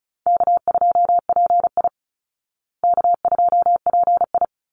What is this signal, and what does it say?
Morse code, heard as a keyed tone, about 34 wpm, 700 Hz. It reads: K2PI K2PI
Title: Easy Call Signs Repeated 3x - 35wpm